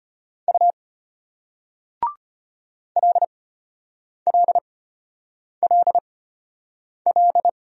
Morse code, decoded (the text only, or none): U LLLL